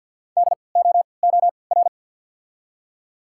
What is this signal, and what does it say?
Morse code, heard as a keyed tone, about 53 wpm, spaced as for 29 wpm, 700 Hz. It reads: DOOR
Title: 3x Character Spacing - Top 300 Words - 50wpm